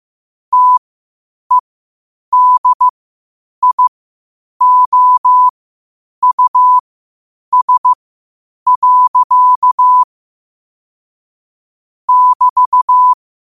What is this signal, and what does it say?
Morse code, heard as a keyed tone, about 15 wpm, spaced as for 8 wpm, 1000 Hz. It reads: TEDIOUS. <BT>
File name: qotdfiles / 8wpm